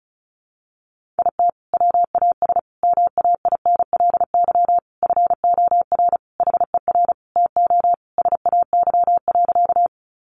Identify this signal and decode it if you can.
Morse code, heard as a keyed tone, about 34 wpm, 700 Hz. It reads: IT WAS MAINLY FOR HER TO SAY.
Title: Sentences from Top 1400 Words 35wpm